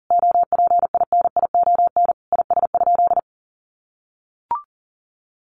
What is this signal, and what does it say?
Morse code, heard as a keyed tone, about 40 wpm, 700 Hz. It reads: OPINION IS?
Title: Sentences from Top 1200 Words 40wpm